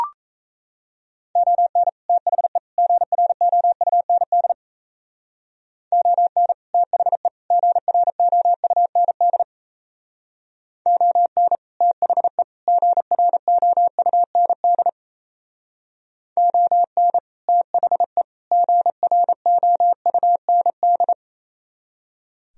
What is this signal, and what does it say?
Morse code, heard as a keyed tone, about 36 wpm, 700 Hz. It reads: ON THE GROUND ON THE GROUND ON THE GROUND ON THE GROUND